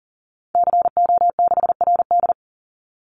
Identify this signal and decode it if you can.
Morse code, heard as a keyed tone, about 38 wpm, 700 Hz. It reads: CO6RD
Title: Call Signs - 40wpm